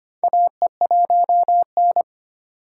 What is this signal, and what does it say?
Morse code, heard as a keyed tone, about 25 wpm, 700 Hz. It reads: AE1N
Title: K1USN SST QSO Practice - 25wpm